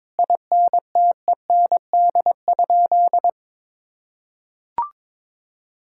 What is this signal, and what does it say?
Morse code, heard as a keyed tone, about 22 wpm, 700 Hz. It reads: INTEND?
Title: Sentences from Top 1600 Words 22wpm